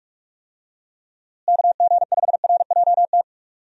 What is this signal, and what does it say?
Morse code, heard as a keyed tone, about 46 wpm, 700 Hz. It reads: KG5RJT
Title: USA Call Signs - 6 Characters - Repeated 2x - 45wpm